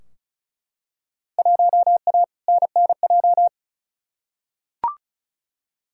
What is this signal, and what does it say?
Morse code, heard as a keyed tone, about 35 wpm, 700 Hz. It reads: 1A NNJ
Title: Field Day Exchanges Encore - 35wpm